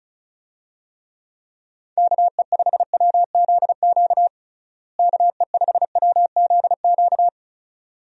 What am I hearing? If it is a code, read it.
Morse code, heard as a keyed tone, about 34 wpm, 700 Hz. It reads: KE5WZQ KE5WZQ
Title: USA Call Signs - 6 Characters - Repeated 2x - 35wpm